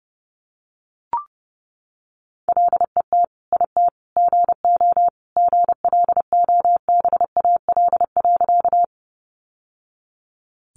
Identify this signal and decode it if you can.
Morse code, heard as a keyed tone, about 30 wpm, 700 Hz. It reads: LET IT GO GLOBAL.
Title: Sentences from Top 1200 Words 30wpm